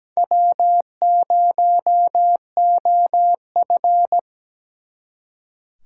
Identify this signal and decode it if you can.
Morse code, heard as a keyed tone, about 20 wpm, 700 Hz. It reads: W0OF
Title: Speed Racing - USA Call Signs 17wpm